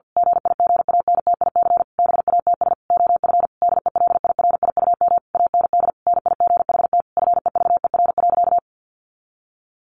Morse code, heard as a keyed tone, about 48 wpm, 700 Hz. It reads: GIGANTIC BATS OF DELIRIUM AND NIGHT FEVER.
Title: A Sound Of Thunder 50wpm